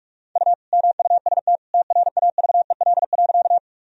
Morse code, heard as a keyed tone, about 46 wpm, 700 Hz. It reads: U MUST TRAVEL.
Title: Sentences from Top 400 Words and Abbreviations 45wpm